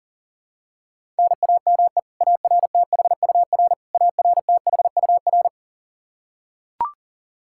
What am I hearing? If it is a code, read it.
Morse code, heard as a keyed tone, about 38 wpm, 700 Hz. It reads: NAME ARTHUR ARTHUR
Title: Practice QSO - Exchange 1 - Beginning - 40wpm